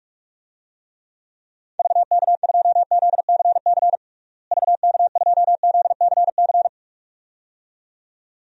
Morse code, heard as a keyed tone, about 42 wpm, 700 Hz. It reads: VK2ZCC VK2ZCC
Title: DXCC-100 Call Signs - 6 Characters - Repeated 2x - 45wpm